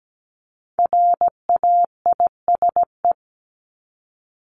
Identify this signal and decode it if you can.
Morse code, heard as a keyed tone, about 17 wpm, 700 Hz. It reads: RAISE
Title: ED Suffix - Top 100 Words with Base Words 17wpm